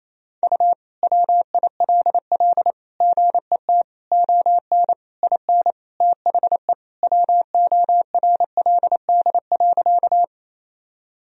Morse code, heard as a keyed tone, about 28 wpm, 700 Hz. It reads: U WILL GET ON IN THE WORLD.
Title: Sentences from Top 200 Words and Abbreviations 28wpm